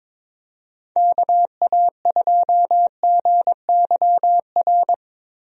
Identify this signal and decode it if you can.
Morse code, heard as a keyed tone, about 22 wpm, 700 Hz. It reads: KA2GYR
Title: USA Call Signs - 6 Characters - Repeated 3x - 22wpm